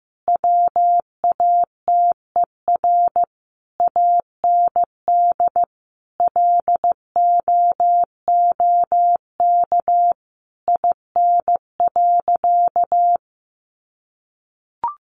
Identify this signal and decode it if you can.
Morse code, heard as a keyed tone, about 15 wpm, 700 Hz. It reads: WATER AND LOOK IN.
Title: Sentences from Top 1300 Words 15wpm